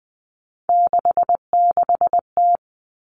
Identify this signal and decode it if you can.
Morse code, heard as a keyed tone, about 20 wpm, 700 Hz. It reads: 66T
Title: Common Cut Numbers - 3-Digit 20wpm